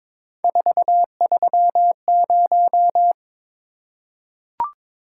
Morse code, heard as a keyed tone, about 22 wpm, 700 Hz. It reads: 430
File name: practice / 22wpm / Three Letter-Number Combinations - 22wpm